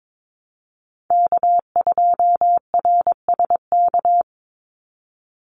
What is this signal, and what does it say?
Morse code, heard as a keyed tone, about 22 wpm, 700 Hz. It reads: K2RSK
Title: Call Signs - 22wpm